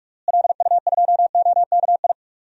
Morse code, heard as a keyed tone, about 46 wpm, 700 Hz. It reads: LU2OKI
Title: DXCC-100 Call Signs - 6 Characters - Repeated 3x - 45wpm